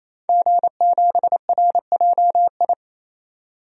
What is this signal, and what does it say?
Morse code, heard as a keyed tone, about 28 wpm, 700 Hz. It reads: G7RJI
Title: DXCC-100 Call Signs - 5 Characters - 28wpm